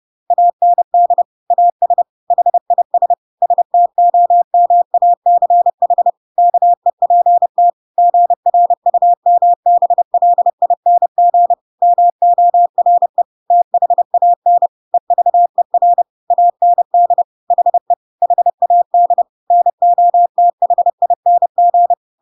Morse code, heard as a keyed tone, about 30 wpm, 700 Hz. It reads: AND AS HIS STOMACH KEPT GRUMBLING MORE THAN EVER AND HE HAD NOTHING
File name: Pinocchio / Pinocchio ch5 30 30 ik1hge